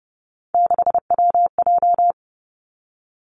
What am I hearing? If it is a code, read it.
Morse code, heard as a keyed tone, about 30 wpm, 700 Hz. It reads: 6WJ
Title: Three Challenging Letter-Number Combinations 30wpm